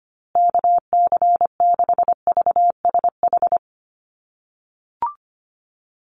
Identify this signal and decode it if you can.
Morse code, heard as a keyed tone, about 25 wpm, 700 Hz. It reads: KC6VSH
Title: USA Call Signs Repeated 2x 25wpm